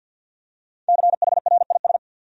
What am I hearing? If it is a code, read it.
Morse code, heard as a keyed tone, about 52 wpm, 700 Hz. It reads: CHRIS
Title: USA Common Names - Max 5 Letters - 50wpm